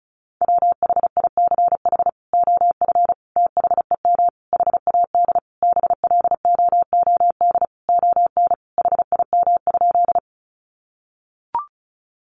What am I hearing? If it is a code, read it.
Morse code, heard as a keyed tone, about 34 wpm, 700 Hz. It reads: WHICH OF THEM HAD BLOOD ON HIM?
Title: Sentences from Top 1100 Words 35wpm